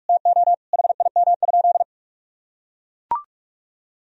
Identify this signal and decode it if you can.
Morse code, heard as a keyed tone, about 41 wpm, 700 Hz. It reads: TO HIM?
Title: Sentences from Top 1000 Words - Review Part 4 - 45wpm